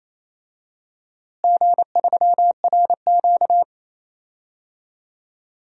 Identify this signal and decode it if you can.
Morse code, heard as a keyed tone, about 27 wpm, 700 Hz. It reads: G3RQ
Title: DXCC-100 Call Signs - 4 Characters - Repeated 3x - 28wpm